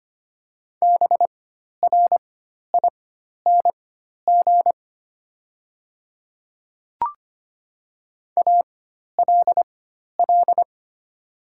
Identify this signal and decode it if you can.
Morse code, heard as a keyed tone, about 25 wpm, spaced as for 12 wpm, 700 Hz. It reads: BRING ALL
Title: 4x Character Spacing - Top 500 Words - 25wpm